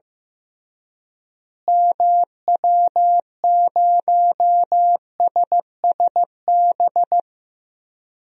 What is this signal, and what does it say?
Morse code, heard as a keyed tone, about 15 wpm, 700 Hz. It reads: MW0SSB
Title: DXCC-100 Call Signs - 6 Characters - Repeated 3x - 15wpm